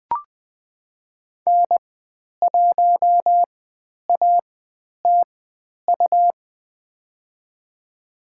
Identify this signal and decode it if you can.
Morse code, heard as a keyed tone, about 20 wpm, spaced as for 10 wpm, 700 Hz. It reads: N1ATU
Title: 109 Lesson 18g - Letter m - Call Signs 20wpm